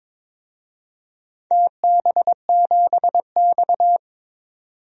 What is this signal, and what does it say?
Morse code, heard as a keyed tone, about 23 wpm, 700 Hz. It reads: TB7X
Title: DXCC-100 Call Signs - 4 Characters - 22wpm